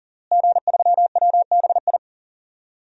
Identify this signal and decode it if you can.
Morse code, heard as a keyed tone, about 37 wpm, 700 Hz. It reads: G3WBI
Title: DXCC-100 Call Signs - 5 Characters - Repeated 3x - 40wpm